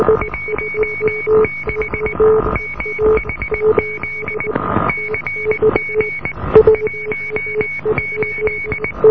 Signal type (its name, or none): Morse code